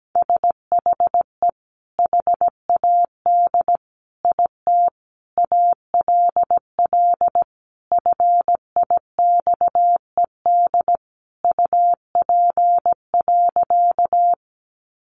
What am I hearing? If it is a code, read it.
Morse code, heard as a keyed tone, about 17 wpm, 700 Hz. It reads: SHE HAD IT ALL FIXED UP.
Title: Sentences from Top 1100 Words 17wpm